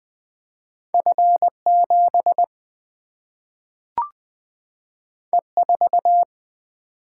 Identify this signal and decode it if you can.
Morse code, heard as a keyed tone, about 20 wpm, 700 Hz. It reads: F 7 E 4
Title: Single-Letter Single-Number 20wpm